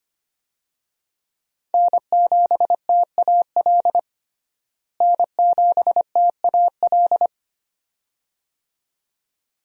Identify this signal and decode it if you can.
Morse code, heard as a keyed tone, about 25 wpm, 700 Hz. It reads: N7TAL N7TAL
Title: USA Call Signs - 5 Characters - Repeated 2x - 25wpm